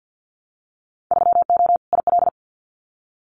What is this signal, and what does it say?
Morse code, heard as a keyed tone, about 50 wpm, 700 Hz. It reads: 3O IL
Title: Winter Field Day Exchanges 50wpm